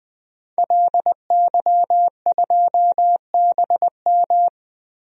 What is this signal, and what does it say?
Morse code, heard as a keyed tone, about 20 wpm, 700 Hz. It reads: LY2BM
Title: DXCC-100 Call Signs - 5 Characters - 20wpm